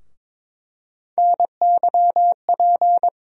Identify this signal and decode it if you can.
Morse code, heard as a keyed tone, about 22 wpm, 700 Hz. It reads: NYP
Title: USA Call Sign Postfixes - 3 Letters 22wpm